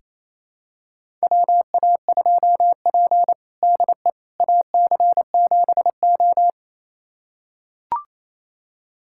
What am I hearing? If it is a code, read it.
Morse code, heard as a keyed tone, about 27 wpm, 700 Hz. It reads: WA2P DE AC7O
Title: USA Callsign DE Callsign - 4 Characters - 28wpm